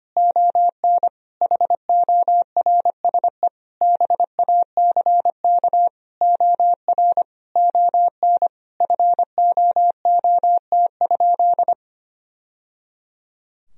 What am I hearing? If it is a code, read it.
Morse code, heard as a keyed tone, about 25 wpm, 700 Hz. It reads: ON HORSE BACK OR ON FOOT?
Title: Sentences from Top 1000 Words - Review Part 3 - 25wpm